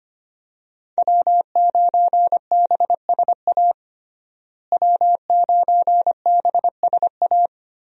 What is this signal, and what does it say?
Morse code, heard as a keyed tone, about 25 wpm, 700 Hz. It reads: W9BSA W9BSA
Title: USA Call Signs - 5 Characters - Repeated 2x - 25wpm